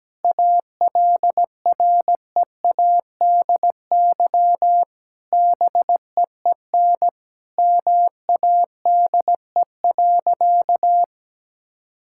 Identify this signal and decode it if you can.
Morse code, heard as a keyed tone, about 17 wpm, 700 Hz. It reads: ALREADY BEEN MADE.
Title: Sentences from Top 1000 Words - Review Part 5 - 17wpm